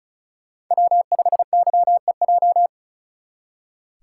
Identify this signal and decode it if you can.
Morse code, heard as a keyed tone, about 43 wpm, 700 Hz. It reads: W5YEJ
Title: Speed Racing - USA 1x3 Call Signs 35wpm